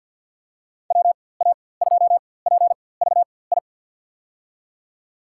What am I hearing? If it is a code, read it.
Morse code, heard as a keyed tone, about 47 wpm, spaced as for 23 wpm, 700 Hz. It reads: WA2PVI
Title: USA Call Signs - 6 Characters - 4x Character Spacing - 50wpm